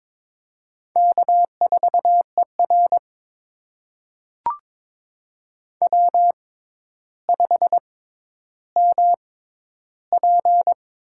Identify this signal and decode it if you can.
Morse code, heard as a keyed tone, about 22 wpm, spaced as for 11 wpm, 700 Hz. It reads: K4ER W5MP
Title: USA Call Signs - 4 Characters - 6x Character Spacing - 22wpm